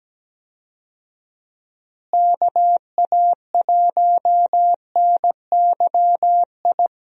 Morse code, heard as a keyed tone, about 17 wpm, 700 Hz. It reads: KA1NYI
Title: USA Call Signs - 6 Characters - Repeated 2x - 17wpm